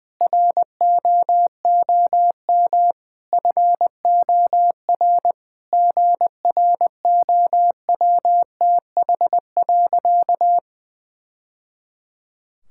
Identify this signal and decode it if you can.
Morse code, heard as a keyed tone, about 20 wpm, 700 Hz. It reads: ROOM FOR GROWTH.
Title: Sentences from Top 1000 Words - Review Part 5 - 20wpm